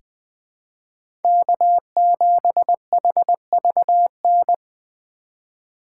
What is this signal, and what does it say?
Morse code, heard as a keyed tone, about 20 wpm, 700 Hz. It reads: K7HVN
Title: USA Call Signs - 5 Characters - 3x Character Spacing - 20wpm